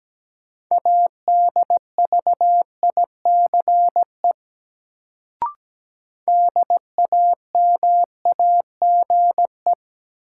Morse code, heard as a keyed tone, about 17 wpm, 700 Hz. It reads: ADVICE DAMAGE
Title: Sentences from Top 1100 Words - New Word Drill Alt - 17wpm